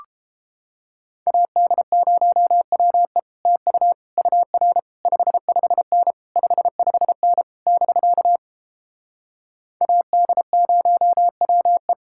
Morse code, heard as a keyed tone, about 29 wpm, 700 Hz. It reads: AD0WE TU UR 55N 55N <BK> AD0WE